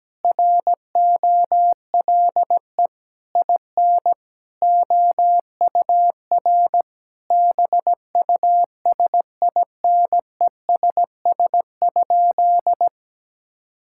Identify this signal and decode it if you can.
Morse code, heard as a keyed tone, about 17 wpm, 700 Hz. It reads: ROLE IN OUR BUSINESS?
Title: Sentences from Top 1000 Words - Review Part 6 - 17wpm